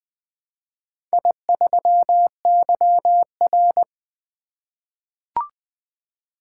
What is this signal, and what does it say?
Morse code, heard as a keyed tone, about 20 wpm, 700 Hz. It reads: I3YR
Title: Four Letter-Number Combinations - 20wpm